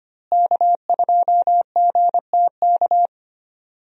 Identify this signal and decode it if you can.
Morse code, heard as a keyed tone, about 25 wpm, 700 Hz. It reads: K2GTK
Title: USA Call Signs - 5 Characters - Repeated 3x - 25wpm